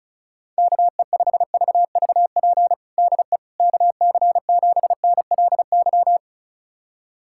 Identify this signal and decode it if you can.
Morse code, heard as a keyed tone, about 36 wpm, 700 Hz. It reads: KE5VVP DE KC7NLY
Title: USA Callsign DE Callsign 35wpm